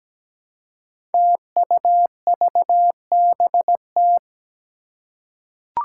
Morse code, heard as a keyed tone, about 17 wpm, 700 Hz. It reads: TUVBT
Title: Five Challenging Letter-Number Combinations 17wpm